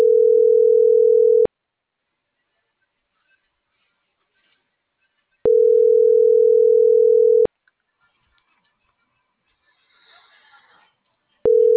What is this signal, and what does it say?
Ringback tone (North America), 440 + 480 Hz.